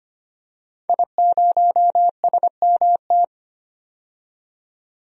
Morse code, heard as a keyed tone, about 25 wpm, 700 Hz. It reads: I0SMT
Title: DXCC-100 Call Signs - 5 Characters - 25wpm